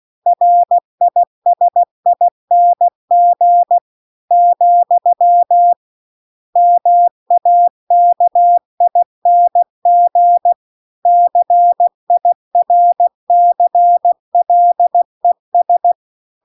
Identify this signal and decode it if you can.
Morse code, heard as a keyed tone, about 16 wpm, 700 Hz. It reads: RISING , MAKING CIRCLES